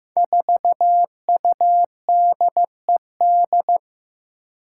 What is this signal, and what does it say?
Morse code, heard as a keyed tone, about 15 wpm, 700 Hz. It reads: 4UDED
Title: Five Challenging Letter-Number Combinations 15wpm